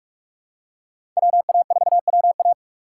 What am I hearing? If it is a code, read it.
Morse code, heard as a keyed tone, about 46 wpm, 700 Hz. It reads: WA4WA
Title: USA Call Signs - 5 Characters - 6x Character Spacing - 45wpm